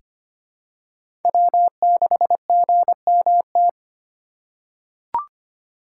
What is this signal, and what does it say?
Morse code, heard as a keyed tone, about 25 wpm, 700 Hz. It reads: W6GMT
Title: USA Call Signs - 5 Characters - 3x Character Spacing - 25wpm